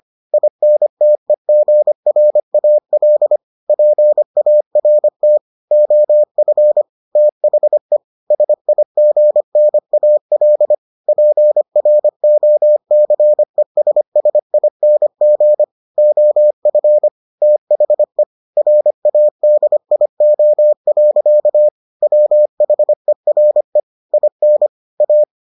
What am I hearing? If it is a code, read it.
Morse code, heard as a keyed tone, about 25 wpm, 600 Hz. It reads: INTEGRAL PART OF THE SIGNAL PROCESSING OF THE RADIO. WHERE IN A